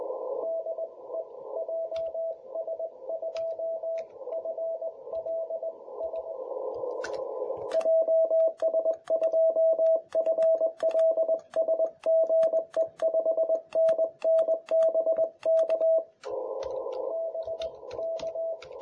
Morse code, heard as a keyed tone, about 18 wpm, 650 Hz, with radio noise. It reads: DE JS2FLS JS2FLS GE 5NN BK BK
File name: jh8cbh / 2021cw